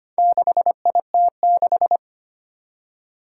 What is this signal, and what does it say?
Morse code, heard as a keyed tone, about 25 wpm, 700 Hz. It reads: 6IT6
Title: Four Challenging Letter-Number Combinations 25wpm